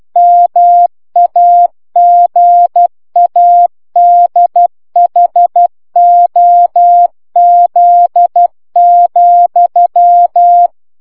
Morse code, heard as a keyed tone, about 12 wpm, 700 Hz. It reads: MAGADHOZ,